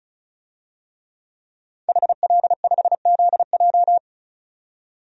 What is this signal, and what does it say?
Morse code, heard as a keyed tone, about 36 wpm, 700 Hz. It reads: HL5ZJ
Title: DXCC-100 Call Signs - 5 Characters - 35wpm